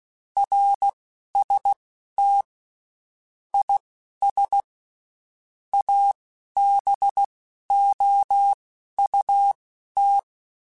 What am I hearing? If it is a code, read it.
Morse code, heard as a keyed tone, about 16 wpm, spaced as for 11 wpm, 800 Hz. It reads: RST IS ABOUT
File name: (spaced) / QSOs Spaced-mp3s / WD0FBT 1 @15 sp